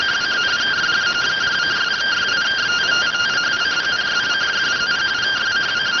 Two-tone teleprinter signal, about 170 Hz shift.